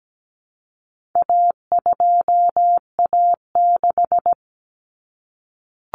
Morse code, heard as a keyed tone, about 17 wpm, 700 Hz. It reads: A2A6
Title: Common Cut Numbers - 4-Digit 17wpm